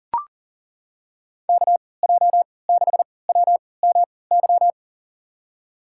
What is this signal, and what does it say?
Morse code, heard as a keyed tone, about 38 wpm, spaced as for 22 wpm, 700 Hz. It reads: KJ6WMY
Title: USA Call Signs - 6 Characters - 3x Character Spacing - 40wpm